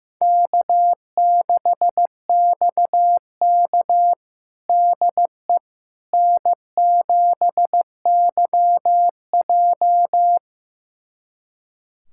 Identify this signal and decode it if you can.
Morse code, heard as a keyed tone, about 15 wpm, 700 Hz. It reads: K6XK DE N7YJ
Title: USA Callsign DE Callsign - 4 Characters - 15wpm